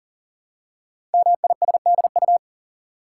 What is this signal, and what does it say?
Morse code, heard as a keyed tone, about 39 wpm, 700 Hz. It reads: MISDU
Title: Code Groups - 5 Letters - 40wpm